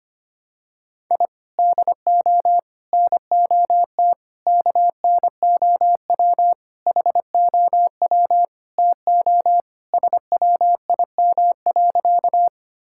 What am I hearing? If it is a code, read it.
Morse code, heard as a keyed tone, about 25 wpm, 700 Hz. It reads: I DO NOT KNOW HOW TO SWIM.
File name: practice / 25wpm / Sentences with 3-Word Phrases - 25wpm